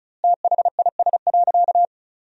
Morse code, heard as a keyed tone, about 36 wpm, 700 Hz. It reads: THIS.
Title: 2x Word-Spacing - Sentences from Top 100 Words - 35wpm